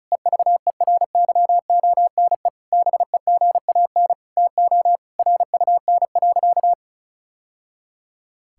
Morse code, heard as a keyed tone, about 36 wpm, 690 Hz. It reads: EVERYONE BEGAN TO RUN.
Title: Sentences from Top 1000 Words - Review Part 5 - 35wpm